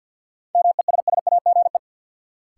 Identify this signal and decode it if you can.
Morse code, heard as a keyed tone, about 53 wpm, 700 Hz. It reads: MESSAGE
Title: Sentences from Top 1100 Words - New Word Drill - 50wpm